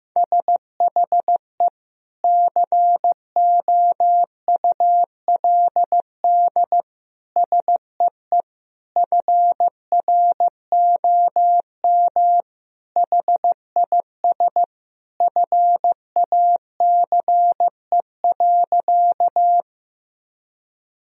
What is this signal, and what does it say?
Morse code, heard as a keyed tone, about 15 wpm, 700 Hz. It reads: SHE COULD SEE FROM HIS FACE.
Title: Sentences from Top 1000 Words - Review Part 2 - 15wpm